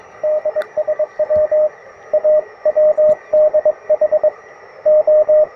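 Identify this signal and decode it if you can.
Morse code, heard as a keyed tone, about 22 wpm, 610 Hz, with radio noise. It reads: DSW A WDH O
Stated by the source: Morse on 20 m during a QRP forest field day, with broadband noise after a magnetic storm and deep, very slow fading.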